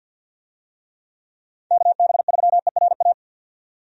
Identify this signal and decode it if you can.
Morse code, heard as a keyed tone, about 51 wpm, 700 Hz. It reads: KB3ERA
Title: USA Call Signs - 6 Characters - 50wpm